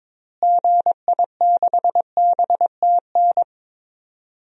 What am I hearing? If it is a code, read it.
Morse code, heard as a keyed tone, about 22 wpm, 700 Hz. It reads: GI6BTN
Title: DXCC-100 Call Signs - 6 Characters - Repeated 3x - 22wpm